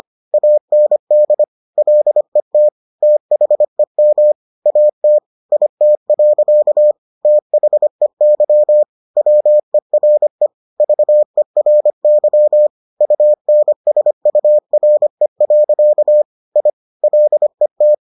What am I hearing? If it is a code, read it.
Morse code, heard as a keyed tone, about 25 wpm, 600 Hz. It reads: AND LET THEM AT IT. THEY WERE VERY UNSURE. I LET